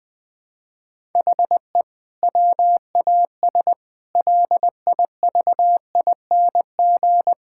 Morse code, heard as a keyed tone, about 21 wpm, 700 Hz. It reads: HE WAS LIVING
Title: Sentences with 6-Word Phrases - 20wpm